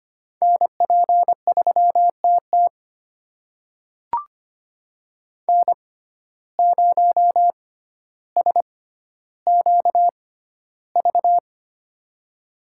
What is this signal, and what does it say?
Morse code, heard as a keyed tone, about 25 wpm, spaced as for 13 wpm, 700 Hz. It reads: NP3TT N0SQV
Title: USA Call Signs - 5 Characters - 6x Character Spacing - 25wpm